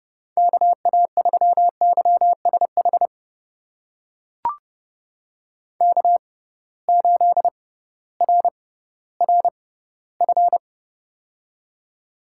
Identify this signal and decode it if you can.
Morse code, heard as a keyed tone, about 30 wpm, spaced as for 16 wpm, 700 Hz. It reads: KA3YSH K8RRF
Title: USA Call Signs - 6x Character Spacing - 30wpm